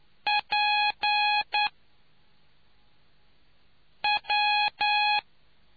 Morse code, heard as a keyed tone, about 9 wpm, 830 Hz. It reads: P W